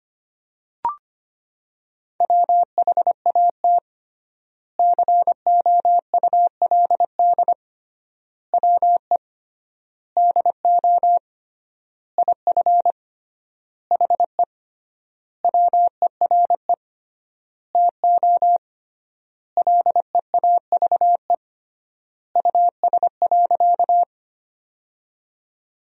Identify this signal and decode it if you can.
Morse code, heard as a keyed tone, about 25 wpm, 700 Hz. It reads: WHAT COULD WE DO IF HE WERE TO LEAVE US.
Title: 3x Word-Spacing - Sentences from Top 400 Words - 25wpm